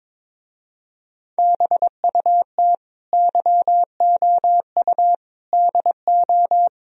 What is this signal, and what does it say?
Morse code, heard as a keyed tone, about 23 wpm, 700 Hz. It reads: BUT YOU DO
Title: Speed-Racing - Sets of 4 Words 22wpm